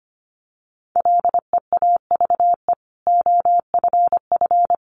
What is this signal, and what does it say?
Morse code, heard as a keyed tone, about 25 wpm, 700 Hz. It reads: LEAVE OFF
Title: Sentences from Top 400 Words 25wpm